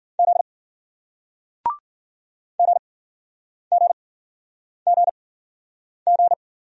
Morse code, heard as a keyed tone, about 56 wpm, spaced as for 9 wpm, 700 Hz. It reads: Z GGGG